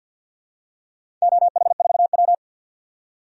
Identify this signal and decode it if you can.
Morse code, heard as a keyed tone, about 49 wpm, 700 Hz. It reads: OH4W
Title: DXCC-100 Call Signs - 4 Characters - 50wpm